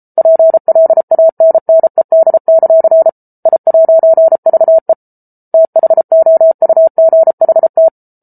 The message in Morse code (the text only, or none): PLANNED; I'VE THOUGHT